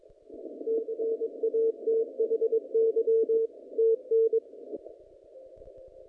Morse code, heard as a keyed tone, about 22 wpm, 450 Hz. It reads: CATHY TN
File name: contests / naqp-cw-k5tr-op-k5pi- -0300